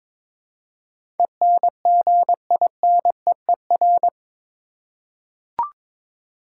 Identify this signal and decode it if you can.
Morse code, heard as a keyed tone, about 22 wpm, 700 Hz. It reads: ENGINEER
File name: practice / 22wpm / Eight Letter Words - 22wpm